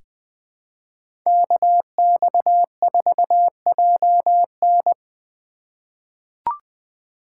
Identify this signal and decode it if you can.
Morse code, heard as a keyed tone, about 20 wpm, 700 Hz. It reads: KX4JN
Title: USA Call Signs - 5 Characters - 4x Character Spacing - 20wpm